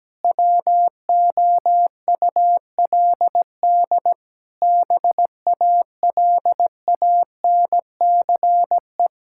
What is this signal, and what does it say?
Morse code, heard as a keyed tone, about 17 wpm, 700 Hz. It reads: WOULD BALANCE
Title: Sentences from Top 1000 Words - Review Part 6 - 17wpm